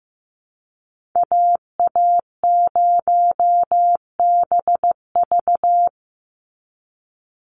Morse code, heard as a keyed tone, about 15 wpm, 700 Hz. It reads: AA0BV
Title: NA SOTA Activator Callsigns 15wpm